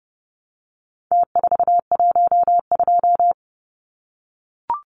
Morse code, heard as a keyed tone, about 29 wpm, 700 Hz. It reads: T412
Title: Numbers - Contest Style - 4-Digit 30wpm